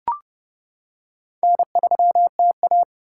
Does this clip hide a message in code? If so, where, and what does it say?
Yes: Morse code from 1.4 to 2.8 s, heard as a keyed tone, about 30 wpm, 700 Hz. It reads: N3TA